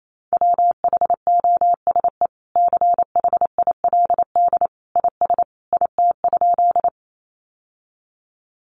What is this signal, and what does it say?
Morse code, heard as a keyed tone, about 28 wpm, 700 Hz. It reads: WHOSE CHILD IS IT?
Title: Sentences from Top 1200 Words 28wpm